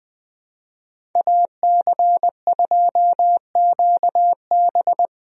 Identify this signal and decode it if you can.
Morse code, heard as a keyed tone, about 21 wpm, 700 Hz. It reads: AC2QB
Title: USA Call Signs - 5 Characters - Repeated 2x - 20wpm